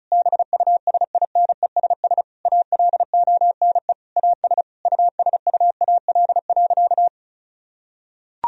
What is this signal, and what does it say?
Morse code, heard as a keyed tone, about 36 wpm, 700 Hz. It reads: BUSINESS ALONE AS USUAL.
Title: Sentences from Top 1000 Words - Review Part 2 - 35wpm